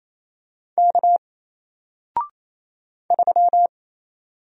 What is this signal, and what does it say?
Morse code, heard as a keyed tone, about 27 wpm, 700 Hz. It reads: K 3
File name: Single-Letter-Number-28wpm